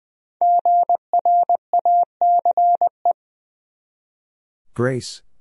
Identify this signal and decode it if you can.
Morse code, heard as a keyed tone, about 21 wpm, 700 Hz. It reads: GRACE
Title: USA Common Names - 20wpm